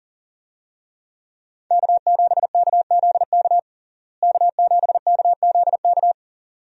Morse code, heard as a keyed tone, about 40 wpm, 700 Hz. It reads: K7KZK K7KZK
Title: USA Call Signs - 5 Characters - Repeated 2x - 40wpm